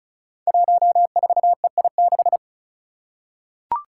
Morse code, heard as a keyed tone, about 34 wpm, 700 Hz. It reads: 14EI6
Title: Five Challenging Letter-Number Combinations 35wpm